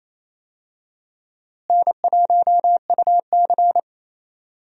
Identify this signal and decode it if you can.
Morse code, heard as a keyed tone, about 28 wpm, 700 Hz. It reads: N1UC
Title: USA Call Signs - 4 Characters - Repeated 2x - 28wpm